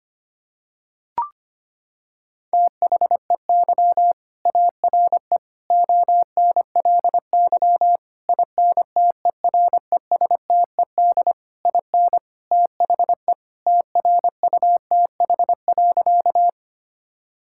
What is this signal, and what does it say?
Morse code, heard as a keyed tone, about 25 wpm, 700 Hz. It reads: THEY ARE ONLY INTERESTED IN THE TRUTH.
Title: Sentences from Top 1000 Words - Review Part 6 - 25wpm